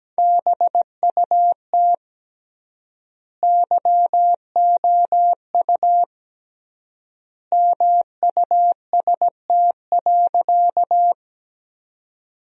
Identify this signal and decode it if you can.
Morse code, heard as a keyed tone, about 17 wpm, 700 Hz. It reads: BUT YOU MUST.
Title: 3x Word-Spacing - Sentences from Top 200 Words - 17wpm